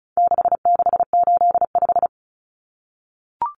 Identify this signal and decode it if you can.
Morse code, heard as a keyed tone, about 34 wpm, 700 Hz. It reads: 6685
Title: Common Cut Numbers - 4-Digit 35wpm